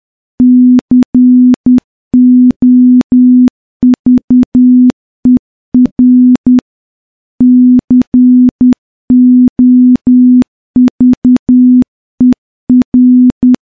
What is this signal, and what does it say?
Morse code, heard as a keyed tone, about 10 wpm, 250 Hz. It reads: COVER COVER
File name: morse-training-10WPM